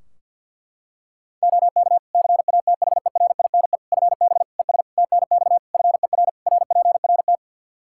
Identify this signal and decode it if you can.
Morse code, heard as a keyed tone, about 52 wpm, 700 Hz. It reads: OK CATHERINE FB ES TNX FER RPRT
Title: Practice QSO - Exchange 2 - Info - 50wpm